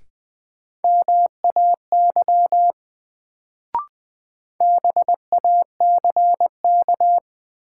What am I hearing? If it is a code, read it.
Morse code, heard as a keyed tone, about 20 wpm, 700 Hz. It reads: MAY BACK
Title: Sets of 2 Words - New Word Drill - 20wpm